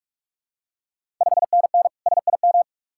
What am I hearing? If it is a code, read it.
Morse code, heard as a keyed tone, about 46 wpm, 690 Hz. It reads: 5NN SIM
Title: Ontario QSO Party Exchange 45wpm